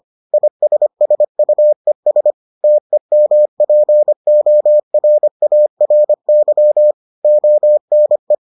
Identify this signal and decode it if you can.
Morse code, heard as a keyed tone, about 25 wpm, 600 Hz. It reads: ISSUES TEMPORARY ONE